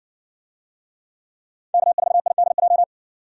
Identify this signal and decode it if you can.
Morse code, heard as a keyed tone, about 60 wpm, 700 Hz. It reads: K4IDJ